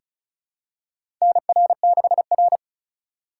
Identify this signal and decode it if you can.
Morse code, heard as a keyed tone, about 36 wpm, 700 Hz. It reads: NR6R
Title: USA Call Signs - 4 Characters - 3x Character Spacing - 35wpm